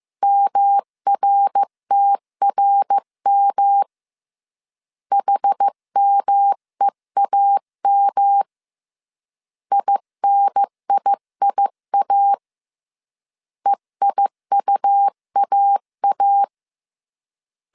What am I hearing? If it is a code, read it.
Morse code, heard as a keyed tone, about 15 wpm, 800 Hz. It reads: MRTRM HMEAM INIIA EIUAA
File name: CW / DET / 1 3 1500C S A 14wpm 2447